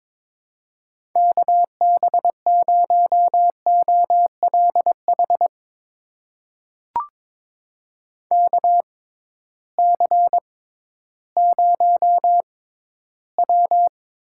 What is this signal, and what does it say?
Morse code, heard as a keyed tone, about 22 wpm, spaced as for 13 wpm, 700 Hz. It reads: KB0OLH KC0W
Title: USA Call Signs - 6x Character Spacing - 22wpm